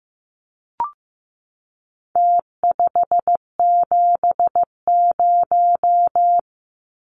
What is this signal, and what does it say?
Morse code, heard as a keyed tone, about 15 wpm, 700 Hz. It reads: T570
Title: Numbers - Contest Style - 4-Digit 15wpm